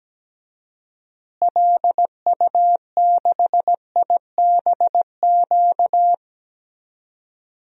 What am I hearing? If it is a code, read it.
Morse code, heard as a keyed tone, about 17 wpm, 700 Hz. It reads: LU6IBQ
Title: DXCC-100 Call Signs - 6 Characters - Repeated 2x - 17wpm